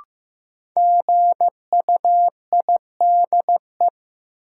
Morse code, heard as a keyed tone, about 15 wpm, 700 Hz. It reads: GUIDE
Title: Sentences from Top 600 Words - New Word Drill - 15wpm